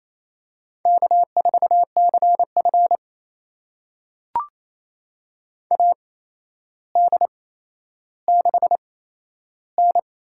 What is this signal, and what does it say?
Morse code, heard as a keyed tone, about 27 wpm, spaced as for 12 wpm, 700 Hz. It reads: K4CF AD6N
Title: USA Call Signs - 4 Characters - 8x Character Spacing - 28wpm